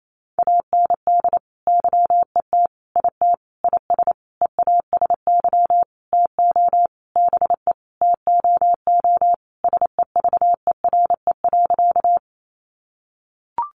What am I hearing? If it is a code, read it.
Morse code, heard as a keyed tone, about 28 wpm, 700 Hz. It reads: AND YET IT IS EASY TO BE TOO SEVERE.
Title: Sentences from Top 1400 Words 28wpm